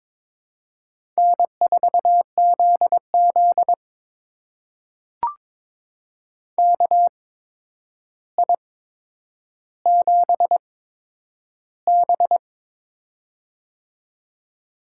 Morse code, heard as a keyed tone, about 22 wpm, spaced as for 9 wpm, 700 Hz. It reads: N4ZZ KI7B
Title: USA Call Signs - 4 Characters - 8x Character Spacing - 22wpm